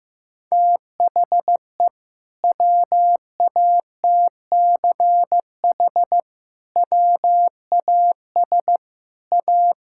Morse code, heard as a keyed tone, about 14 wpm, 700 Hz. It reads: THE WATCH WAS A
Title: Sentences from Top 1000 Words - Review Part 6 - 15wpm